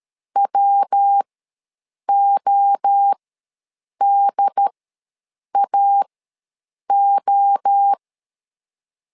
Morse code, heard as a keyed tone, about 13 wpm, spaced as for 7 wpm, 800 Hz. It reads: WODAO
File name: LEZIONI / 3 MWRO / 3 MWRO ws 12wpm 1914